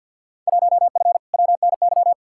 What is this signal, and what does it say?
Morse code, heard as a keyed tone, about 51 wpm, 700 Hz. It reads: 1F WNY
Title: Field Day Exchanges - 50wpm